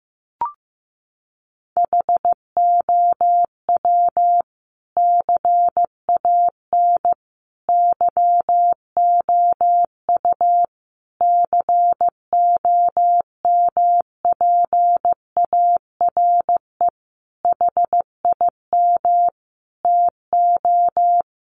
Morse code, heard as a keyed tone, about 15 wpm, 700 Hz. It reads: HOW CAN YOU COMPARE HIM TO
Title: Sentences from Top 1300 Words 15wpm